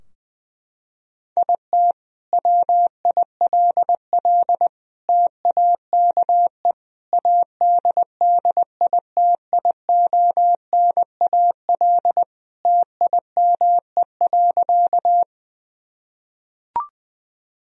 Morse code, heard as a keyed tone, about 20 wpm, 700 Hz. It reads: IT WILL TAKE ADDITIONAL TIME.
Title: Sentences from Top 1000 Words - Review Part 4 - 20wpm